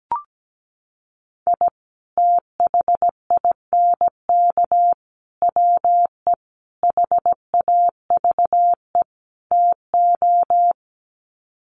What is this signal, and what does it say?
Morse code, heard as a keyed tone, about 17 wpm, 700 Hz. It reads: I THINK WE HAVE TO
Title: Sets of 5 Words 17wpm